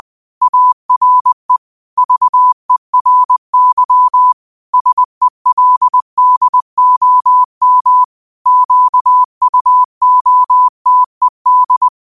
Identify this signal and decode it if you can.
Morse code, heard as a keyed tone, about 20 wpm, 1000 Hz. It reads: ARE VERY SELDOM QUOTED